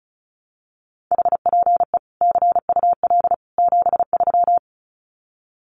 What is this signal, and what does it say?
Morse code, heard as a keyed tone, about 34 wpm, 700 Hz. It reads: HPE CUL 73
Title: Common QSO Phrases 35wpm